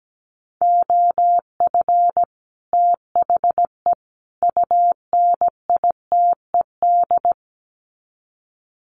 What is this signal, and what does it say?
Morse code, heard as a keyed tone, about 17 wpm, 700 Hz. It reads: OF THE UNITED
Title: Sets of 3 Words 17wpm